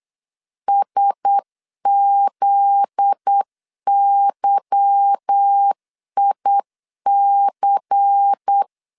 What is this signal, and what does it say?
Morse code, heard as a keyed tone, about 8 wpm, 800 Hz. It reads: SZYIC
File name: LEZIONI / 9 286 8wpm 3425